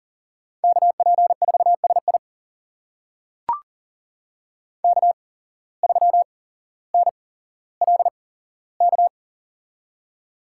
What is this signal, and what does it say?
Morse code, heard as a keyed tone, about 42 wpm, spaced as for 17 wpm, 700 Hz. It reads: KP4SI K3NLK
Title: USA Call Signs - 5 Characters - 8x Character Spacing - 40wpm